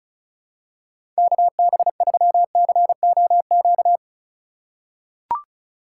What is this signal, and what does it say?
Morse code, heard as a keyed tone, about 36 wpm, 700 Hz. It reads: KB3COQ
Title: USA Call Signs - 4x Character Spacing - 35wpm